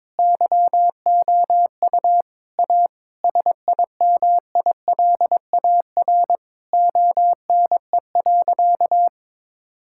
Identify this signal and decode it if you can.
Morse code, heard as a keyed tone, about 23 wpm, 700 Hz. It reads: YOU A SIMILAR ONE.
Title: Sentences from Top 1000 Words - Review Part 5 - 22wpm